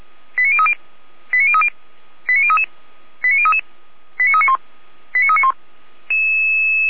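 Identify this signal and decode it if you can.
Five-tone selective call (ZVEI), heard as quick runs of five tones, tones of about 70 ms. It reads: 89030 89030 89033 89033 89391 89391
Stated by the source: ZVEI five-tone call 89050 received on SDR.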